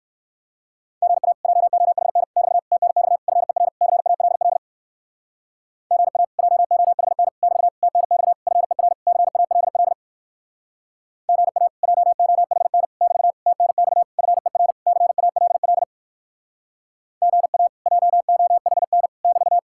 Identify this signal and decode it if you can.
Morse code, heard as a keyed tone, about 58 wpm, 700 Hz. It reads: GA JOHN <BT> TNX FER CALL GA JOHN <BT> TNX FER CALL GA JOHN <BT> TNX FER CALL GA JOHN <BT>